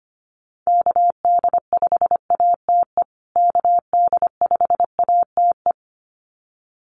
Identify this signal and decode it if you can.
Morse code, heard as a keyed tone, about 25 wpm, 700 Hz. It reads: KD5ATE KD5ATE
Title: USA Call Signs Repeated 2x 25wpm